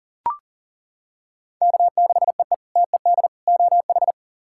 Morse code, heard as a keyed tone, about 40 wpm, 700 Hz. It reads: K6EE TED OH
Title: K1USN SST QSO Practice - 40wpm